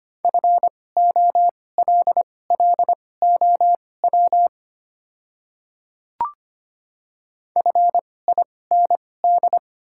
Morse code, heard as a keyed tone, about 25 wpm, spaced as for 18 wpm, 690 Hz. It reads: FOLLOW FIND
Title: 2x Character Spacing - Top 200 Words - 25wpm